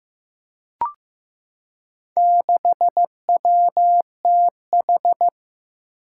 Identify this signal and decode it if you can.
Morse code, heard as a keyed tone, about 15 wpm, 700 Hz. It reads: 6WTH
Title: Four Challenging Letter-Number Combinations 15wpm